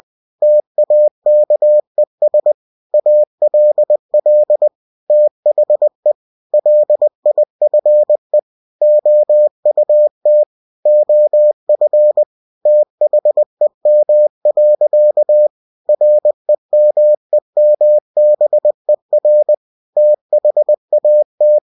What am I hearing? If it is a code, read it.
Morse code, heard as a keyed tone, about 20 wpm, 600 Hz. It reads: TAKES ALL THE LIFE OUT OF THEM. REMEMBER THAT
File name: cw-books / winnie-the-pooh / winnie-20wpm-page-0084